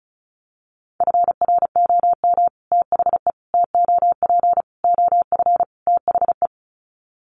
Morse code, heard as a keyed tone, about 34 wpm, 700 Hz. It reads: FROM THE TOP OF THE
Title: Sets of 5 Words Encore 35wpm